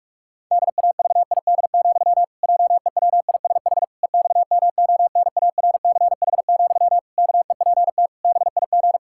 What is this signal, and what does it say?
Morse code, heard as a keyed tone, about 41 wpm, 700 Hz. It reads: DAVID, JEWISH EXMONARCH, KEPT BIG
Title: Pangram Challenge v2 - 45wpm